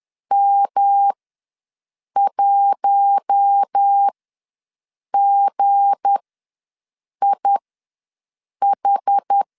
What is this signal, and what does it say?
Morse code, heard as a keyed tone, about 10 wpm, spaced as for 6 wpm, 800 Hz. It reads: M1GIH
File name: LEZIONI / 7 GF9Z / 7 G ws 10wpm 3843